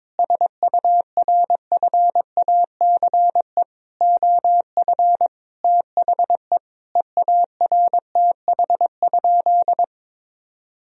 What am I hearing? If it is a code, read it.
Morse code, heard as a keyed tone, about 22 wpm, 700 Hz. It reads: SURFACE OF THE EARTH?
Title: Sentences from Top 1000 Words - Review Part 3 - 22wpm